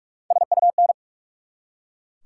Morse code, heard as a keyed tone, about 44 wpm, 700 Hz. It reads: SUN
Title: Three Letter Words - 45wpm